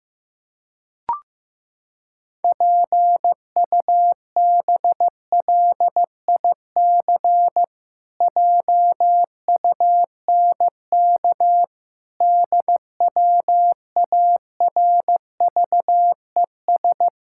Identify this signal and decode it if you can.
Morse code, heard as a keyed tone, about 15 wpm, 700 Hz. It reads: PUBLIC JUNK DWARVES
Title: Pangram Challenge v2 - 15wpm